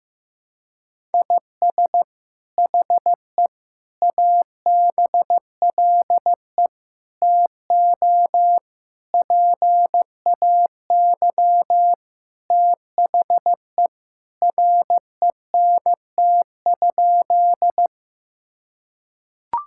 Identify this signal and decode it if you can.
Morse code, heard as a keyed tone, about 16 wpm, 700 Hz. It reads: IS HE ABLE TO PAY THE RENT?
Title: Sentences from Top 1000 Words - Review Part 5 - 15wpm